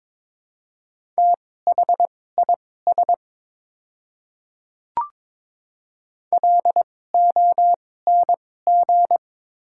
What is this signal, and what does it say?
Morse code, heard as a keyed tone, about 22 wpm, spaced as for 16 wpm, 700 Hz. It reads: THIS LONG
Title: 2x Character Spacing - Top 100 Words - 22wpm